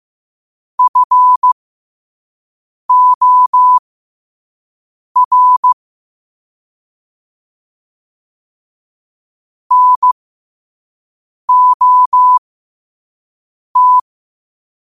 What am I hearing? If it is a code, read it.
Morse code, heard as a keyed tone, about 15 wpm, spaced as for 5 wpm, 1000 Hz. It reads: FOR NOT
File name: qotdfiles / 5wpm